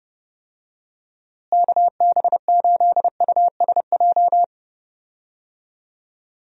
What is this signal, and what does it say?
Morse code, heard as a keyed tone, about 30 wpm, 700 Hz. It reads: KB8USJ
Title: USA Call Signs - 6 Characters - Repeated 3x - 30wpm